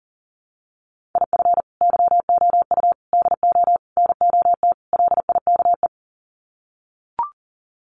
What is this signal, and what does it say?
Morse code, heard as a keyed tone, about 40 wpm, 700 Hz. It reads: IF YOU DO NOT LIKE
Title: Sets of 5 Words 40wpm